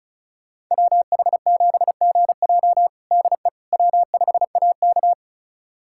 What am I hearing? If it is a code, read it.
Morse code, heard as a keyed tone, about 36 wpm, 700 Hz. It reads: WH7GJ DE W5AK
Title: USA Callsign DE Callsign 35wpm